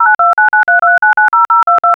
DTMF, heard as telephone touch tones, tones of about 140 ms. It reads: 09293239022
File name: smart-phone-dial-call-sou-ek4njvm7